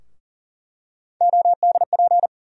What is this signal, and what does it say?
Morse code, heard as a keyed tone, about 40 wpm, 700 Hz. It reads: ODP
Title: USA Call Sign Postfixes - 3 Letters 40wpm